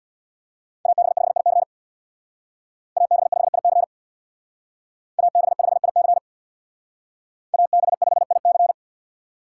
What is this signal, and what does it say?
Morse code, heard as a keyed tone, about 61 wpm, 700 Hz. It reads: AB5IC AB5IC AB5IC AB5IC
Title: Speed Racing - USA 2x2 Call Signs 50wpm